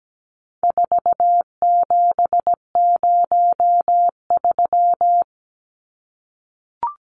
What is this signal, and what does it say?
Morse code, heard as a keyed tone, about 17 wpm, 700 Hz. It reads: 4703
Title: Numbers - 4-Digit 17wpm